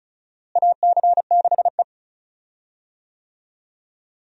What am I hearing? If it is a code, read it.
Morse code, heard as a keyed tone, about 36 wpm, 700 Hz. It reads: AC6E
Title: USA Call Signs - 4 Characters - Repeated 2x - 35wpm